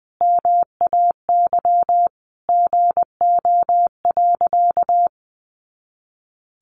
Morse code, heard as a keyed tone, about 19 wpm, 700 Hz. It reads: MAY GO.
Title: Sentences from Top 100 Words 20wpm